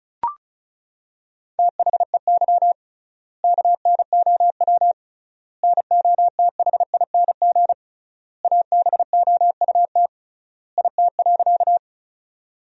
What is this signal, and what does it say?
Morse code, heard as a keyed tone, about 34 wpm, 700 Hz. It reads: THEY KNOW NOTHING ABOUT IT.
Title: 3x Word-Spacing - Sentences from Top 500 Words - 35wpm